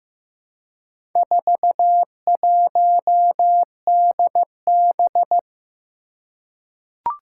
Morse code, heard as a keyed tone, about 15 wpm, 700 Hz. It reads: 41DB
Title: Four Challenging Letter-Number Combinations 15wpm